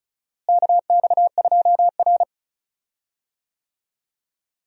Morse code, heard as a keyed tone, about 36 wpm, 700 Hz. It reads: KX2R
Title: USA Call Signs - 4 Characters - Repeated 3x - 35wpm